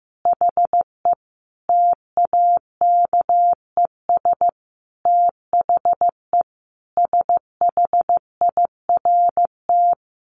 Morse code, heard as a keyed tone, about 15 wpm, 700 Hz. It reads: HE TAKES THE SHIRT
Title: Additional Easy Phrases 15wpm